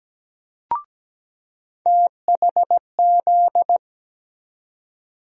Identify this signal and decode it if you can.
Morse code, heard as a keyed tone, about 17 wpm, 700 Hz. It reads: THZ
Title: USA Call Sign Postfixes - 3 Letters 17wpm